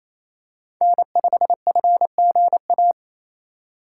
Morse code, heard as a keyed tone, about 27 wpm, 700 Hz. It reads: N5FGA
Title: USA Call Signs - 2x Character Spacing - 28wpm